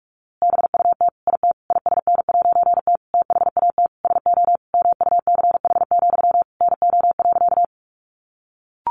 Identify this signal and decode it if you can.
Morse code, heard as a keyed tone, about 44 wpm, 700 Hz. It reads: BUT IT ISN'T THAT SO MUCH, NO.
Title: A Sound Of Thunder 45wpm